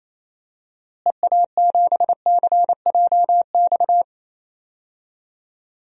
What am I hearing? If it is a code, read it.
Morse code, heard as a keyed tone, about 27 wpm, 700 Hz. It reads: EA7CJX
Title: DXCC-100 Call Signs - 6 Characters - Repeated 3x - 28wpm